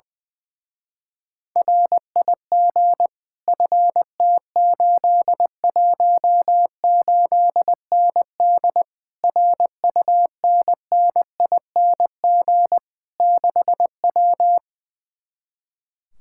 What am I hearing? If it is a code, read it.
Morse code, heard as a keyed tone, about 20 wpm, 700 Hz. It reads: RIG FT818ND RUNNING 6W
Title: Speed-Racing Common QSO Phrases Encore - 20wpm